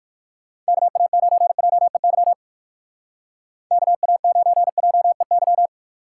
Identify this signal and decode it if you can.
Morse code, heard as a keyed tone, about 58 wpm, 700 Hz. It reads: KA9JEY KA9JEY